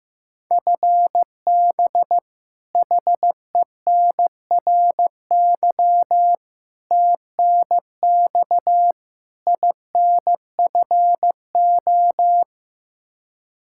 Morse code, heard as a keyed tone, about 15 wpm, 700 Hz. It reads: FB HENRY TNX INFO
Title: Practice QSO - Exchange 3 - More Info - 15wpm